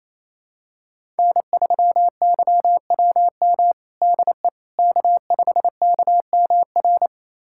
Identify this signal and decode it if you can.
Morse code, heard as a keyed tone, about 28 wpm, 700 Hz. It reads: N3YWM DE K5KMR
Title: USA Callsign DE Callsign - 5 Characters - 28wpm